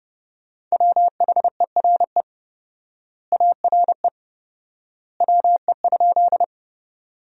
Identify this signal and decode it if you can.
Morse code, heard as a keyed tone, about 30 wpm, 700 Hz. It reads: WHERE ARE WE?
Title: 4x Word-Spacing - Sentences from Top 200 Words - 30wpm